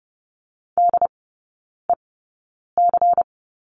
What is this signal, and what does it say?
Morse code, heard as a keyed tone, about 30 wpm, spaced as for 9 wpm, 690 Hz. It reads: DEC
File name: Lesson 16e - Letter c - Three Characters 30wpm